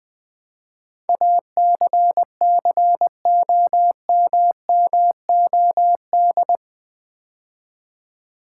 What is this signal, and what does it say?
Morse code, heard as a keyed tone, about 20 wpm, 700 Hz. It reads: ACCOMMOD
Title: Build A Word - 4 Letters At A Time - 11-Letter Words - 20wpm